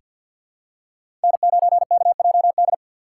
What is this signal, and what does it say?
Morse code, heard as a keyed tone, about 49 wpm, 700 Hz. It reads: N9KJD
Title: USA Call Signs - 5 Characters - 4x Character Spacing - 50wpm